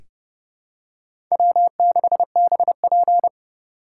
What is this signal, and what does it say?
Morse code, heard as a keyed tone, about 30 wpm, 700 Hz. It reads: W6BP
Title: USA Call Signs - 3x Character Spacing - 30wpm